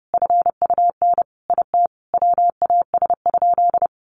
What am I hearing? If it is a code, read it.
Morse code, heard as a keyed tone, about 30 wpm, 700 Hz. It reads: FUN IT WAS?
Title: Sentences from Top 700 Words 30wpm